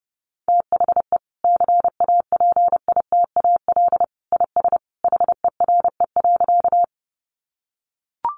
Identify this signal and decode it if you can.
Morse code, heard as a keyed tone, about 30 wpm, 700 Hz. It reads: THE CAPITAL IS HERE.
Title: Sentences from Top 1000 Words 30wpm